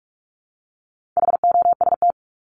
Morse code, heard as a keyed tone, about 44 wpm, 700 Hz. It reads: HOST
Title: Top 1501-1750 Words 45wpm